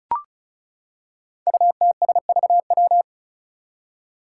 Morse code, heard as a keyed tone, about 36 wpm, 700 Hz. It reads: UTSVW
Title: Five Challenging Letter-Number Combinations 35wpm